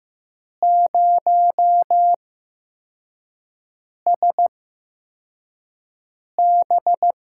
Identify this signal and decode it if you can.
Morse code, heard as a keyed tone, about 15 wpm, spaced as for 4 wpm, 700 Hz. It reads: 0SB
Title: USA Call Signs - 8x Character Spacing - 15wpm